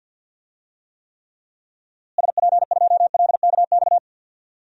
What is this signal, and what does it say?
Morse code, heard as a keyed tone, about 52 wpm, 700 Hz. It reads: SP2LKX
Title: DXCC-100 Call Signs - 6 Characters - Repeated 3x - 50wpm